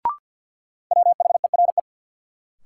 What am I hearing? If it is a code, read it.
Morse code, heard as a keyed tone, about 53 wpm, 700 Hz. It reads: WHERE
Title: Sentences from Top 200 Words - New Word Drill - 50wpm